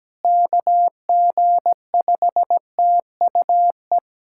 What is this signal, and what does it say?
Morse code, heard as a keyed tone, about 17 wpm, 700 Hz. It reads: KG5TUE
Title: USA Callsign DE Callsign 17wpm